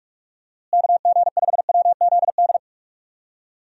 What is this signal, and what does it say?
Morse code, heard as a keyed tone, about 41 wpm, 700 Hz. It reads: KG5WZD
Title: USA Call Signs - 6 Characters - Repeated 2x - 45wpm